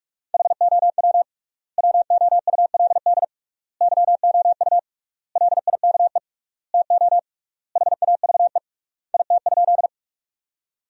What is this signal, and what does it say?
Morse code, heard as a keyed tone, about 46 wpm, 700 Hz. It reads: HOW WOULD YOU LIKE TO HAVE IT?
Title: 3x Word-Spacing - Sentences from Top 100 Words - 45wpm